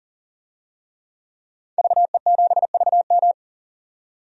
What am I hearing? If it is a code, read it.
Morse code, heard as a keyed tone, about 37 wpm, 700 Hz. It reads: VE7VM
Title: DXCC-100 Call Signs - 5 Characters - 40wpm